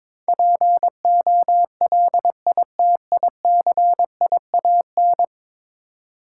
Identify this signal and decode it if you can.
Morse code, heard as a keyed tone, about 22 wpm, 700 Hz. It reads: POLITICIAN
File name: practice / 22wpm / 10 Letter Words - First 3 Letters Given - 22wpm